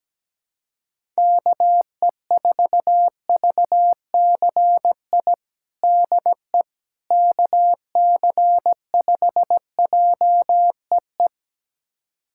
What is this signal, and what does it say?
Morse code, heard as a keyed tone, about 17 wpm, 700 Hz. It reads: KE4VCI DE KC5JEE
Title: USA Callsign DE Callsign - 6 Characters - 17wpm